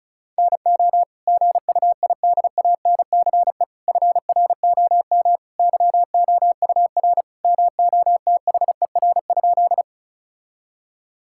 Morse code, heard as a keyed tone, about 34 wpm, 700 Hz. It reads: NO GUIDANCE FROM YOUR MOTHER?
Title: Sentences from Top 1600 Words 35wpm